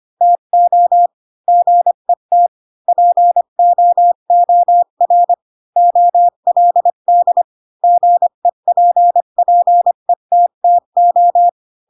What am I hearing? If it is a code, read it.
Morse code, heard as a keyed tone, about 25 wpm, 700 Hz. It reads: TO GET POOR OLD GEPPETTO
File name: Pinocchio / Pinocchio ch4 20 25 ik1hge